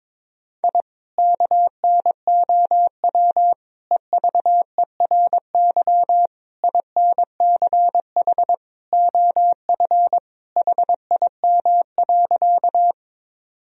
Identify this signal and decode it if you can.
Morse code, heard as a keyed tone, about 23 wpm, 700 Hz. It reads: I KNOW EVERY INCH OF HIM.
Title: Sentences from Top 1000 Words - Review Part 3 - 22wpm